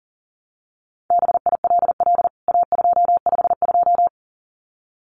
Morse code, heard as a keyed tone, about 37 wpm, 700 Hz. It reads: BILL A252
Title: CWT Exchange 40wpm